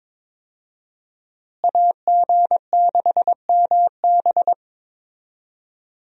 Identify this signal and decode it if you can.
Morse code, heard as a keyed tone, about 22 wpm, 700 Hz. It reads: AG6MB
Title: USA Call Signs - 5 Characters - Repeated 2x - 22wpm